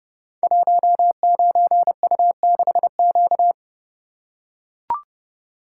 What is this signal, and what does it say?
Morse code, heard as a keyed tone, about 30 wpm, 700 Hz. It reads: 19U6Q
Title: Code Groups - 5 Letter:Numbers - 30wpm